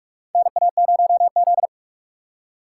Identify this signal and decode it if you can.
Morse code, heard as a keyed tone, about 44 wpm, 700 Hz. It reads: NA0Z
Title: USA Call Signs - 4 Characters - 4x Character Spacing - 45wpm